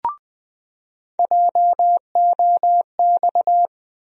Morse code, heard as a keyed tone, about 20 wpm, 700 Hz. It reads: JOX
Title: USA Call Sign Postfixes - 3 Letters 20wpm